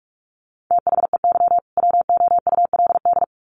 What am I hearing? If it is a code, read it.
Morse code, heard as a keyed tone, about 44 wpm, 700 Hz. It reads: THEY WOULD
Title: Sets of 2 Words 45wpm